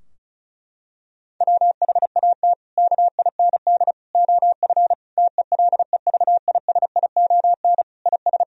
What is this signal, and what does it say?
Morse code, heard as a keyed tone, about 36 wpm, 700 Hz. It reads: WHAT KIND OF TELEVISION IS
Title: Sentences from Top 1000 Words - Review Part 4 - 35wpm